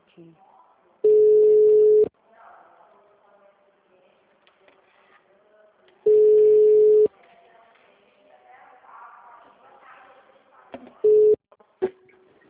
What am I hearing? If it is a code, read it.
Ringback tone (Europe, 425 Hz), with speech.
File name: out- -2016- - - .11486